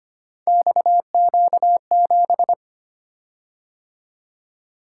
Morse code, heard as a keyed tone, about 25 wpm, 700 Hz. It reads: XQ7
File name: Three Letter-Number Combinations - 25wpm